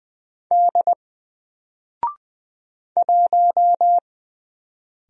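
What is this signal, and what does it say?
Morse code, heard as a keyed tone, about 20 wpm, 700 Hz. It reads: D 1
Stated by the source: Single letter and number Morse drill at 20 wpm, from N4LCD.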